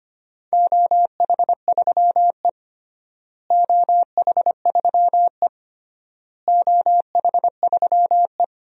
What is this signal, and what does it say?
Morse code, heard as a keyed tone, about 25 wpm, 700 Hz. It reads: OH3E OH3E OH3E
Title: DXCC-100 Call Signs - 4 Characters - Repeated 3x - 25wpm